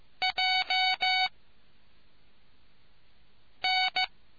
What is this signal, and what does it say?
Morse code, heard as a keyed tone, about 15 wpm, 750 Hz. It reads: J N